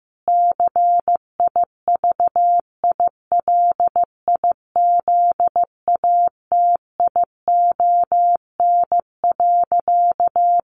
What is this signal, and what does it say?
Morse code, heard as a keyed tone, about 15 wpm, 700 Hz. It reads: CIVILIZATION.
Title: A Sound Of Thunder 15wpm